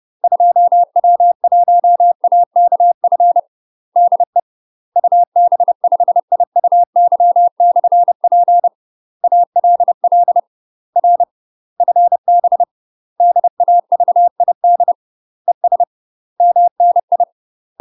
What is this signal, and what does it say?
Morse code, heard as a keyed tone, about 30 wpm, 700 Hz. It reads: 2W1AKF DE UB5IUY/P ALL R FB DAVID ES MNI
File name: UK MASTER QSOs TRAINING / 30WPM / UK MASTER QSOs 0049